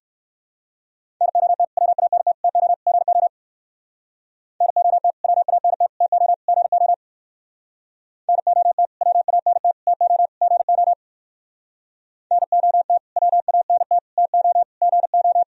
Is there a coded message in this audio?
Yes: Morse code from 1.2 to 15.4 s, heard as a keyed tone, about 58 wpm, 700 Hz. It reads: NOT WANT TO GO NOT WANT TO GO NOT WANT TO GO NOT WANT TO GO